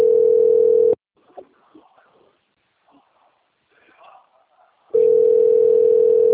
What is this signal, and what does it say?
Ringback tone (North America), 440 + 480 Hz, with speech.
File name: out- - - - - .582